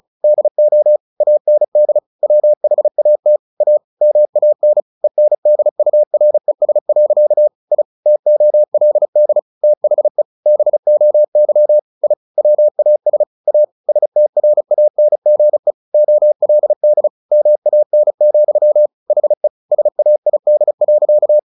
Morse code, heard as a keyed tone, about 34 wpm, 600 Hz. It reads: DO AND WHAT A MAN ENDURES. I TOLD THE BOY I WAS A STRANGE OLD MAN, HE SAID.
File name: the old man and the sea / the old man and the sea 1 35wpm